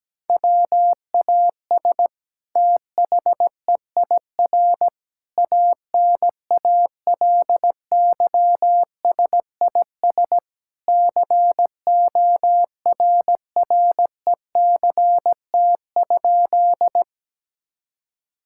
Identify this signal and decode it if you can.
Morse code, heard as a keyed tone, about 17 wpm, 700 Hz. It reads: WAS THEIR ANALYSIS CORRECT?
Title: Sentences from Top 1000 Words - Review Part 5 - 17wpm